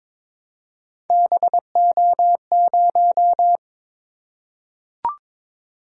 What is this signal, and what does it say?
Morse code, heard as a keyed tone, about 23 wpm, 700 Hz. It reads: BO0
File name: practice / 22wpm / Three Letter-Number Combinations - 22wpm